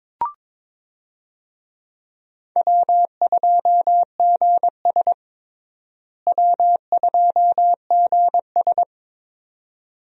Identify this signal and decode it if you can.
Morse code, heard as a keyed tone, about 22 wpm, 700 Hz. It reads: W2GS W2GS
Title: USA Call Signs - 4 Characters - Repeated 2x - 22wpm